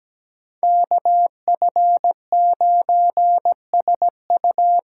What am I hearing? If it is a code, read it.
Morse code, heard as a keyed tone, about 20 wpm, 700 Hz. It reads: KF9SU
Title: Speed Racing - USA 2x2 Call Signs 17wpm